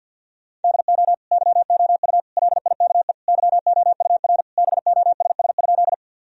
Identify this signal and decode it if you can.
Morse code, heard as a keyed tone, about 48 wpm, 700 Hz. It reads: DO YOU LIKE YOUR BOSS?
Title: Sentences from Top 1000 Words - Review Part 4 - 50wpm